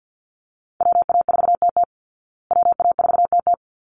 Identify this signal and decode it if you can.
Morse code, heard as a keyed tone, about 50 wpm, 700 Hz. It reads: WA4TT WA4TT
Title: Call Signs Repeated 3x - 50wpm